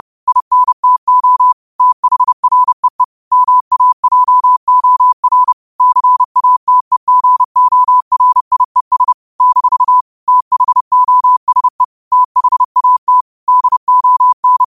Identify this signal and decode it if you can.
Morse code, heard as a keyed tone, about 30 wpm, 1000 Hz. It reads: INTO THREE MAJOR CATEGORIES - THOSE THAT DON